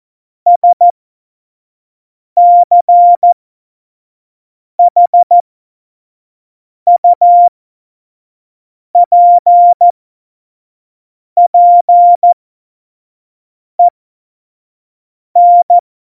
Morse code, heard as a keyed tone, about 14 wpm, spaced as for 5 wpm, 700 Hz. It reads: SCHUPPEN
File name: Morsekurs / Witze / Witz 30. Oktober 2018, 25 F70